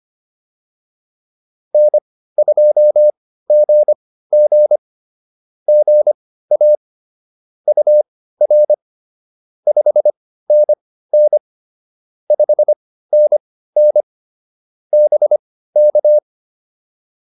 Morse code, heard as a keyed tone, about 25 wpm, spaced as for 15 wpm, 600 Hz. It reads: N2GG GA UR 5NN 5NN BK
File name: POTA104 15